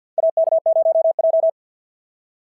Morse code, heard as a keyed tone, about 51 wpm, 660 Hz. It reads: AK0J
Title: SOTA QSO Practice 50wpm